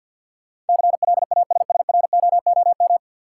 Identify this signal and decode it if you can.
Morse code, heard as a keyed tone, about 51 wpm, 700 Hz. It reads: CLASSROOM
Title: Sentences from Top 1500 Words - New Word Drill Alt - 50wpm